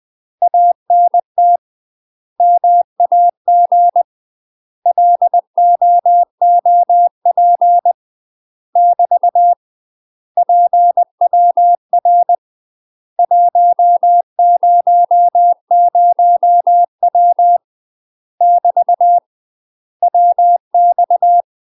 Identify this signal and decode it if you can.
Morse code, heard as a keyed tone, about 20 wpm, 700 Hz. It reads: ANT MAG LOOP <BT> PWR 100W <BT> WX
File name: UK MASTER QSOs TRAINING / 20WPM / UK MASTER QSOs 0011